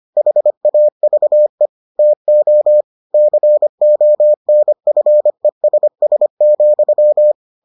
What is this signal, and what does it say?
Morse code, heard as a keyed tone, about 25 wpm, 600 Hz. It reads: HAVE TO CONFESS,